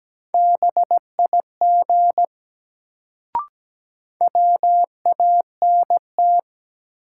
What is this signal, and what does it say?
Morse code, heard as a keyed tone, about 17 wpm, 700 Hz. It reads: BIG WANT
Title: Sentences from Top 200 Words - New Word Drill - 17wpm